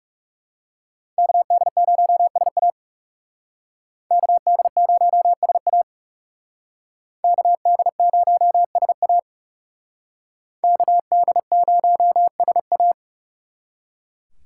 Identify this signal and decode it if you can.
Morse code, heard as a keyed tone, about 36 wpm, 700 Hz. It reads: KD0SA KD0SA KD0SA KD0SA
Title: Speed Racing - USA 2x2 Call Signs 30wpm